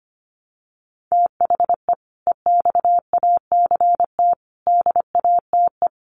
Morse code, heard as a keyed tone, about 25 wpm, 700 Hz. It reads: THE EXACT DATE
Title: Sentences from Top 1300 Words 25wpm